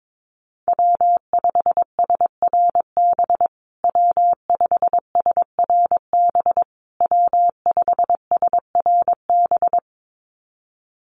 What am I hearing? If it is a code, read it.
Morse code, heard as a keyed tone, about 22 wpm, 700 Hz. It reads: W5SRB W5SRB W5SRB
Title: USA Call Signs Repeated 3x 22wpm